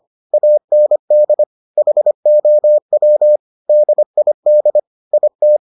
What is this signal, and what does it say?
Morse code, heard as a keyed tone, about 25 wpm, 600 Hz. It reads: AND HOW DID IT